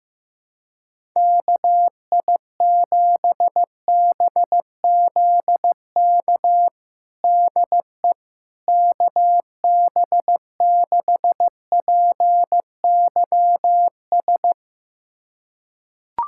Morse code, heard as a keyed tone, about 15 wpm, 700 Hz. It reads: KI7BZK DE KB6PYS
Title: USA Callsign DE Callsign - 6 Characters - 15wpm